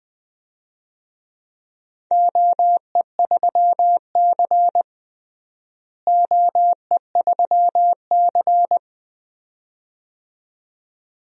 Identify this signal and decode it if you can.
Morse code, heard as a keyed tone, about 20 wpm, 700 Hz. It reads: OE3C OE3C
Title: DXCC-100 Call Signs - 4 Characters - Repeated 2x - 20wpm